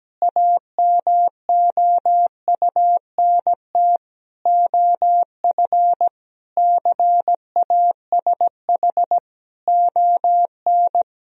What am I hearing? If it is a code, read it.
Morse code, heard as a keyed tone, about 17 wpm, 700 Hz. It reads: AMOUNT OF CASH ON
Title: Sentences from Top 1000 Words - Review Part 4 - 17wpm